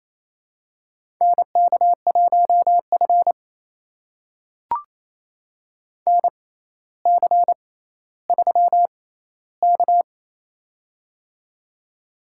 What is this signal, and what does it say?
Morse code, heard as a keyed tone, about 28 wpm, spaced as for 14 wpm, 700 Hz. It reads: NK1F NC3K
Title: USA Call Signs - 4 Characters - 6x Character Spacing - 28wpm